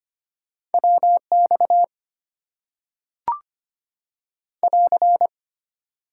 Morse code, heard as a keyed tone, about 25 wpm, 690 Hz. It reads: WX <AR>
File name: Top 90 QSO Elements - 8x Character Spacing - 25wpm